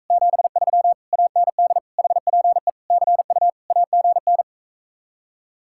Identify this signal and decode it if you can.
Morse code, heard as a keyed tone, about 45 wpm, 700 Hz. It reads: 73 AND HPE CU AGN